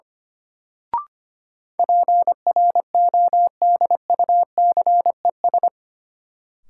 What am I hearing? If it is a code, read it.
Morse code, heard as a keyed tone, about 25 wpm, 700 Hz. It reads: PRODUCES
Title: Sentences from Top 1000 Words - Review Part 6 - New Word Drill - 25wpm